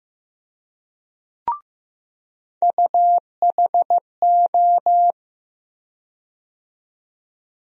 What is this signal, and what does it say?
Morse code, heard as a keyed tone, about 15 wpm, 700 Hz. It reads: UHO
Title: USA Call Sign Postfixes - 3 Letters 15wpm